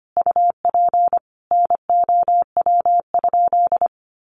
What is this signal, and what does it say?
Morse code, heard as a keyed tone, about 25 wpm, 700 Hz. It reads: UP NOW?
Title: Sentences from Top 1300 Words 25wpm